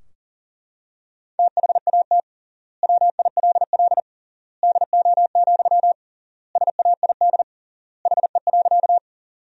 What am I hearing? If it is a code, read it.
Morse code, heard as a keyed tone, about 40 wpm, 700 Hz. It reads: THAT WILL DO, SAID HE.
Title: 3x Word-Spacing - Sentences from Top 100 Words - 40wpm